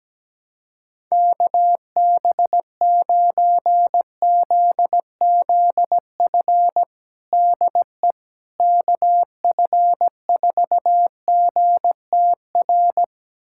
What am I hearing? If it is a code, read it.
Morse code, heard as a keyed tone, about 17 wpm, 700 Hz. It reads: KB9ZZF DE KF4GTR
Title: USA Callsign DE Callsign - 6 Characters - 17wpm